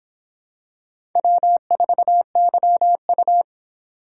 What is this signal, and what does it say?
Morse code, heard as a keyed tone, about 26 wpm, 700 Hz. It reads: W4YU